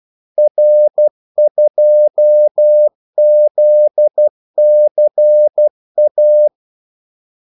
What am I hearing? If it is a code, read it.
Morse code, heard as a keyed tone, about 12 wpm, 600 Hz. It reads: R2ZCA
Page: 2 ekstra ordmellemrum mellem kaldesignalerne